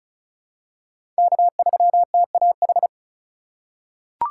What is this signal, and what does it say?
Morse code, heard as a keyed tone, about 36 wpm, 700 Hz. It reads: K3TAH
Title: USA Call Signs - 5 Characters - 2x Character Spacing - 35wpm